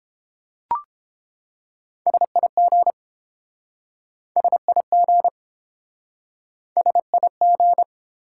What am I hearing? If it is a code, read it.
Morse code, heard as a keyed tone, about 30 wpm, 700 Hz. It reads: SIG SIG SIG